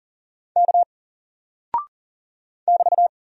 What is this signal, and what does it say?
Morse code, heard as a keyed tone, about 41 wpm, 700 Hz. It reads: K <BT>
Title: Single Character (Letter:Number:Punctuation:Prosign) 40wpm